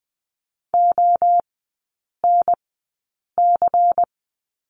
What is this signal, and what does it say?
Morse code, heard as a keyed tone, about 20 wpm, spaced as for 8 wpm, 700 Hz. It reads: ONC
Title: 127 Lesson 21e - Number 6 - Three Characters 20wpm